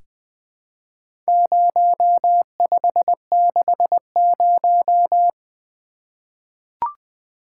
Morse code, heard as a keyed tone, about 20 wpm, 700 Hz. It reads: 0560
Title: Four Letter-Number Combinations - 20wpm